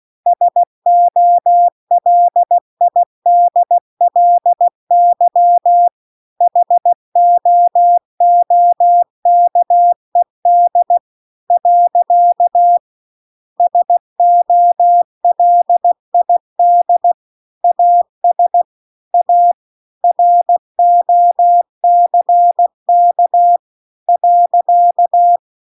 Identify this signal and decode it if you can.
Morse code, heard as a keyed tone, about 16 wpm, 700 Hz. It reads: SOLIDLY HOOKED . SOLID AS A ROCK .
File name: CW-The-Big-Twoheaded-River-2-16WPM